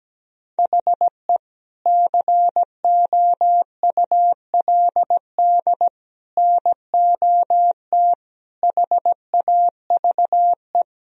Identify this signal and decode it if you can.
Morse code, heard as a keyed tone, about 17 wpm, 700 Hz. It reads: HE COULD NOT HAVE
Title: Sentences from Top 1000 Words - Review Part 2 - 17wpm